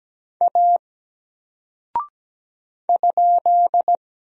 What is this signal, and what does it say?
Morse code, heard as a keyed tone, about 17 wpm, 700 Hz. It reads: A ?
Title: Single Character (Letter:Number:Punctuation:Prosign) 17wpm